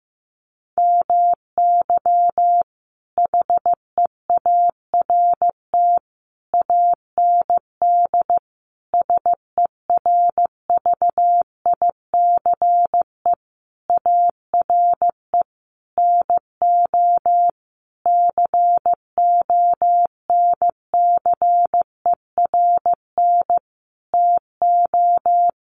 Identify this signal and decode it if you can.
Morse code, heard as a keyed tone, about 15 wpm, 700 Hz. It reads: MY HEART AND SERVICE ARE NO CONCERN TO
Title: Sentences from Top 1200 Words 15wpm